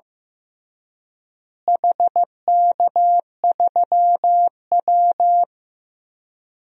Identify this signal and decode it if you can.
Morse code, heard as a keyed tone, about 15 wpm, 700 Hz. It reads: HK3W
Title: DXCC-100 Call Signs - 4 Characters - Repeated 3x - 15wpm